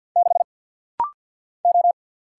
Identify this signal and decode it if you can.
Morse code, heard as a keyed tone, about 48 wpm, 690 Hz. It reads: 6 O
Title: Single Letter-Number Alt - Rapid-Fire 50wpm